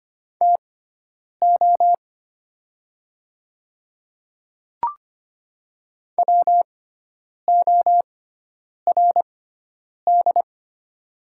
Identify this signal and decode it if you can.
Morse code, heard as a keyed tone, about 25 wpm, spaced as for 9 wpm, 690 Hz. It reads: TO WORD
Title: 6x Character Spacing - Top 200 Words - 25wpm